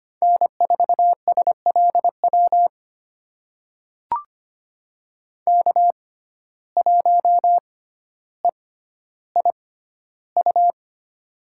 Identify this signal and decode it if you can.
Morse code, heard as a keyed tone, about 25 wpm, spaced as for 13 wpm, 700 Hz. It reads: N4SLW K1EIU
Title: USA Call Signs - 5 Characters - 6x Character Spacing - 25wpm